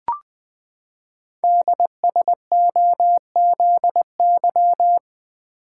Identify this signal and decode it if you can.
Morse code, heard as a keyed tone, about 20 wpm, 700 Hz. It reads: DSOZY
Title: Code Groups - 5 Letters - 20wpm